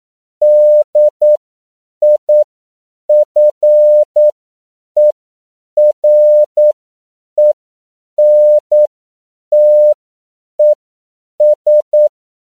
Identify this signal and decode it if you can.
Morse code, heard as a keyed tone, about 9 wpm, spaced as for 7 wpm, 600 Hz. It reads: DIFERENTES
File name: CW / 10PPM